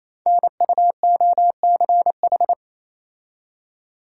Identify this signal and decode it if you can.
Morse code, heard as a keyed tone, about 27 wpm, 700 Hz. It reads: NUOCH
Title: Code Groups - 5 Letters - 28wpm